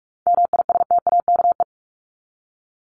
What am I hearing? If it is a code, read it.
Morse code, heard as a keyed tone, about 44 wpm, 700 Hz. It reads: MISTAKE
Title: Top 1501-1750 Words 45wpm